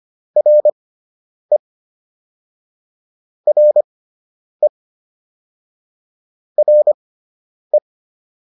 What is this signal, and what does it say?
Morse code, heard as a keyed tone, about 25 wpm, spaced as for 9 wpm, 600 Hz. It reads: RE RE RE